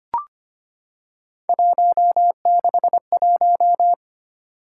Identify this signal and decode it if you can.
Morse code, heard as a keyed tone, about 25 wpm, 700 Hz. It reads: 161
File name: Three Letter-Number Combinations - 25wpm